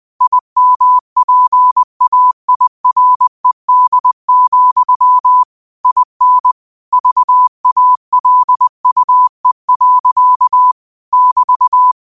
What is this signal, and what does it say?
Morse code, heard as a keyed tone, about 20 wpm, 1000 Hz. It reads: IMPAIRED, IN VALUE. <BT>